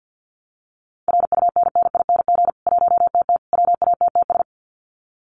Morse code, heard as a keyed tone, about 50 wpm, 700 Hz. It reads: RUNNING 1TT WATTS
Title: Common QSO Phrases 50wpm